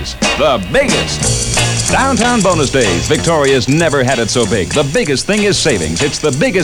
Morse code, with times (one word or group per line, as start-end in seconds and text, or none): none